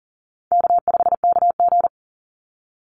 Morse code, heard as a keyed tone, about 41 wpm, 700 Hz. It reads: K5KG
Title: Call Signs - 40wpm